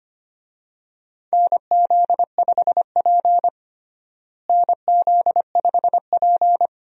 Morse code, heard as a keyed tone, about 25 wpm, 700 Hz. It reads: NZ5P NZ5P
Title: USA Call Signs - 4 Characters - Repeated 2x - 25wpm